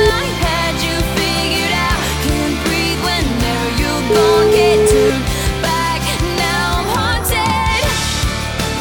Ringback tone (Europe, 425 Hz), with noise.